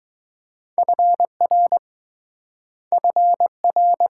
Morse code, heard as a keyed tone, about 20 wpm, 700 Hz. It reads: FR FR